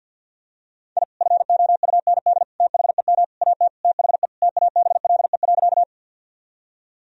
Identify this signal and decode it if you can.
Morse code, heard as a keyed tone, about 52 wpm, 700 Hz. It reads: I FOUND THEM AT THE TABLE.
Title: Sentences from Top 1000 Words - Review Part 2 - 50wpm